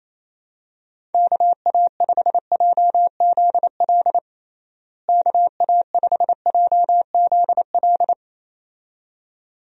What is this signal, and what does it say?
Morse code, heard as a keyed tone, about 28 wpm, 700 Hz. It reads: KA5JZL KA5JZL
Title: USA Call Signs - 6 Characters - Repeated 2x - 28wpm